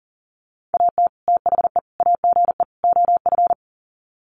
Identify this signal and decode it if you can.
Morse code, heard as a keyed tone, about 40 wpm, 700 Hz. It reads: AT THE AGE OF
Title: Sets of 4 Words 40wpm